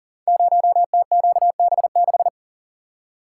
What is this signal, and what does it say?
Morse code, heard as a keyed tone, about 41 wpm, 700 Hz. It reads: 0TQB6
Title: Code Groups - 5 Letter:Numbers - 40wpm